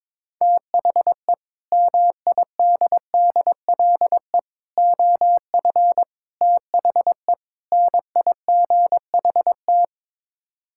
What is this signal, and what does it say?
Morse code, heard as a keyed tone, about 22 wpm, 700 Hz. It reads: THE MIDDLE OF THE NIGHT
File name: practice / 22wpm / Build A Phrase - 5-Word Phrases - 22wpm